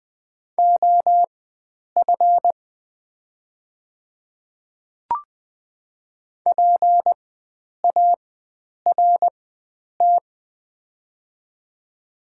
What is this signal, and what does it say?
Morse code, heard as a keyed tone, about 20 wpm, spaced as for 9 wpm, 700 Hz. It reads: OF PART
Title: 4x Character Spacing - Top 300 Words - 20wpm